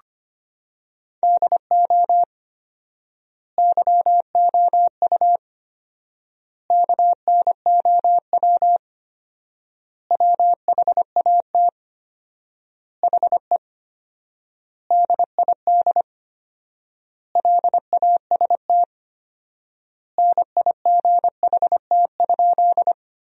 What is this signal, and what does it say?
Morse code, heard as a keyed tone, about 25 wpm, 700 Hz. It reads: DO YOU KNOW WHAT HE DID LAST NIGHT?
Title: 4x Word-Spacing - Sentences from Top 300 Words - 25wpm